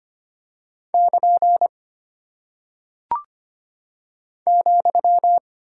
Morse code, heard as a keyed tone, about 25 wpm, 700 Hz. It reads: <KN> ,
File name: practice / 25wpm / Single Punctuation-Prosign - 25wpm